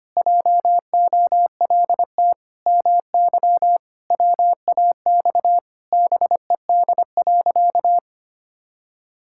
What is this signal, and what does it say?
Morse code, heard as a keyed tone, about 25 wpm, 700 Hz. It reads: JOLT MY WAX BED.
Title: Pangram Challenge v2 - 25wpm